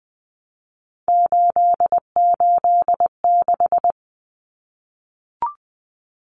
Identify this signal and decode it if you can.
Morse code, heard as a keyed tone, about 20 wpm, 700 Hz. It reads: 886
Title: Numbers - 3-Digit 20wpm